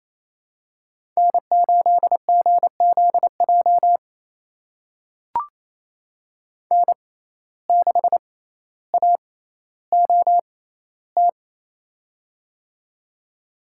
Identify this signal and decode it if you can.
Morse code, heard as a keyed tone, about 28 wpm, spaced as for 14 wpm, 700 Hz. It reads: N8GZJ N6AOT
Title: USA Call Signs - 5 Characters - 6x Character Spacing - 28wpm